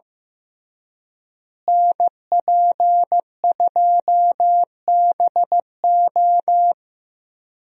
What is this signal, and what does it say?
Morse code, heard as a keyed tone, about 15 wpm, 700 Hz. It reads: NP2BO
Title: USA Call Signs - 5 Characters - Repeated 2x - 15wpm